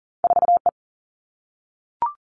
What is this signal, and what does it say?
Morse code, heard as a keyed tone, about 40 wpm, 700 Hz. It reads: 4 E
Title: Two Challenging Letter-Number Combinations 40wpm